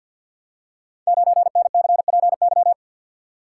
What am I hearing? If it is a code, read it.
Morse code, heard as a keyed tone, about 53 wpm, 700 Hz. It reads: 9NCPY
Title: Code Groups - 5 Letter:Numbers - 50wpm